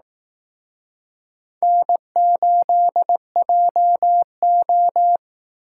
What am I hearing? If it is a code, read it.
Morse code, heard as a keyed tone, about 19 wpm, 700 Hz. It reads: N8JO